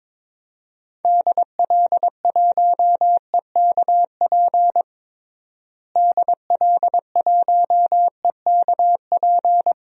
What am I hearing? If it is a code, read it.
Morse code, heard as a keyed tone, about 21 wpm, 700 Hz. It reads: DL1EKP DL1EKP
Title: DXCC-100 Call Signs - 6 Characters - Repeated 2x - 22wpm